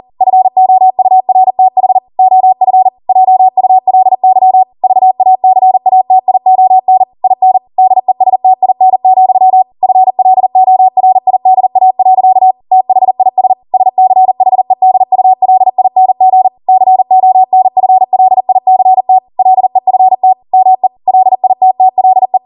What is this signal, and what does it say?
Morse code, heard as a keyed tone, about 40 wpm, 750 Hz. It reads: FOURTH OF JULY VACATION IN DESTIN, FLORIDA. THIS SCHEDULING CONFLICT LEFT ME LITTLE